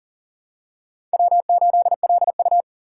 Morse code, heard as a keyed tone, about 40 wpm, 700 Hz. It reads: W8LU
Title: USA Call Signs - 4 Characters - 4x Character Spacing - 40wpm